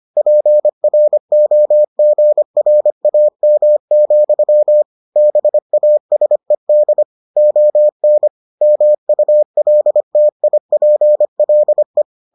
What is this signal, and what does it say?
Morse code, heard as a keyed tone, about 25 wpm, 600 Hz. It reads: PROGRAM, BASED ON MULTIPLE